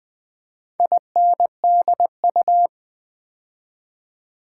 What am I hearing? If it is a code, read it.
Morse code, heard as a keyed tone, about 20 wpm, 700 Hz. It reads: INDU
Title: Build A Word - 2 Letters At A Time - 8-Letter Words - 20wpm